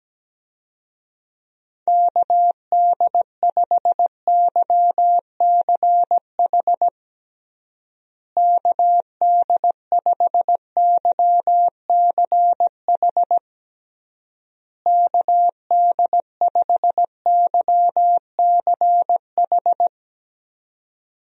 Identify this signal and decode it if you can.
Morse code, heard as a keyed tone, about 17 wpm, 700 Hz. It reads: KD5YCH KD5YCH KD5YCH
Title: USA Call Signs - 6 Characters - Repeated 3x - 17wpm